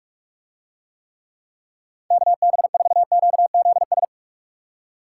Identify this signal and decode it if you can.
Morse code, heard as a keyed tone, about 44 wpm, 700 Hz. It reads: KB4QZS